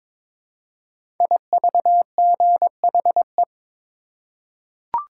Morse code, heard as a keyed tone, about 23 wpm, 700 Hz. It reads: IVGHE
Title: Code Groups - 5 Letters - 22wpm